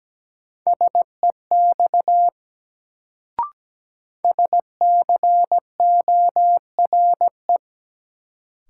Morse code, heard as a keyed tone, about 17 wpm, 700 Hz. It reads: SEX SCORE
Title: Sentences from Top 1600 Words - New Word Drill Alt - 17wpm